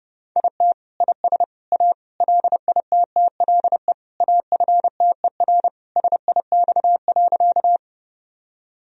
Morse code, heard as a keyed tone, about 32 wpm, 700 Hz. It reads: IT IS A LITTLE AFTER SIX.
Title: Sentences from Top 1000 Words - Review Part 2 - 30wpm